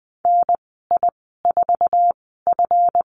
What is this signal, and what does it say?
Morse code, heard as a keyed tone, about 20 wpm, spaced as for 14 wpm, 700 Hz. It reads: NI4F
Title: 161 Lesson 26g - Letter g - Call Signs 20wpm